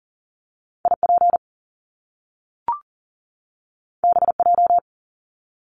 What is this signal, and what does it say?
Morse code, heard as a keyed tone, about 39 wpm, 700 Hz. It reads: IP BJ
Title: Top Two Letter Combinations in Words 40wpm